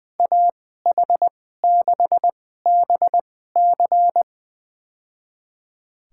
Morse code, heard as a keyed tone, about 20 wpm, spaced as for 14 wpm, 700 Hz. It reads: AH6BC
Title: USA Call Signs - 2x Character Spacing - 20wpm